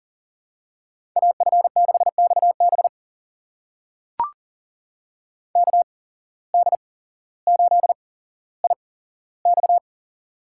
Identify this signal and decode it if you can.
Morse code, heard as a keyed tone, about 40 wpm, spaced as for 17 wpm, 700 Hz. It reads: AF6XB KD8IX
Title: USA Call Signs - 5 Characters - 8x Character Spacing - 40wpm